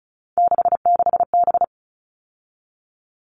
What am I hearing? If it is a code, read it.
Morse code, heard as a keyed tone, about 34 wpm, 700 Hz. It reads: 66B
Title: All Cut Numbers - 3-Digit 35wpm